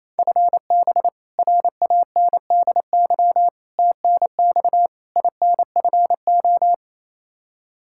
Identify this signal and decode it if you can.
Morse code, heard as a keyed tone, about 28 wpm, 700 Hz. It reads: FB RANDY TNX INFO
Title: Practice QSO - Exchange 3 - More Info - 28wpm